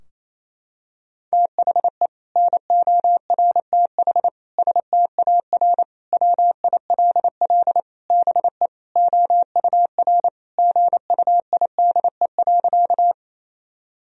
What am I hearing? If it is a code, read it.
Morse code, heard as a keyed tone, about 28 wpm, 700 Hz. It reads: THE NORTH STAR WILL BE OUR GUIDE.
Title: Sentences from Top 1000 Words - Review Part 3 - 28wpm